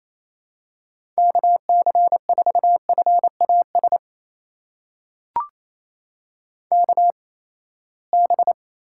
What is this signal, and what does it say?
Morse code, heard as a keyed tone, about 28 wpm, spaced as for 19 wpm, 700 Hz. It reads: KC4FAS KB
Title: USA Call Signs - 6 Characters - 8x Character Spacing - 28wpm